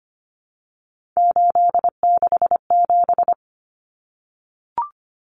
Morse code, heard as a keyed tone, about 25 wpm, 700 Hz. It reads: 867
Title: Common Cut Numbers - 3-Digit 25wpm